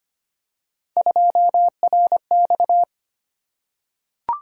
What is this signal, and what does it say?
Morse code, heard as a keyed tone, about 25 wpm, 700 Hz. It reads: 2RX
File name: Three Letter-Number Combinations - 25wpm